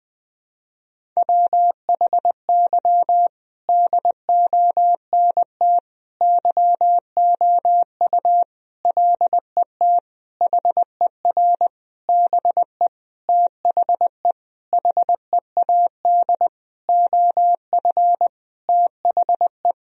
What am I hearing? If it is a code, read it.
Morse code, heard as a keyed tone, about 20 wpm, 700 Hz. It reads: WHY DONT YOU LET HER BE THE HEAD OF THE
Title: Sentences from Top 1000 Words - Review Part 1 - 20wpm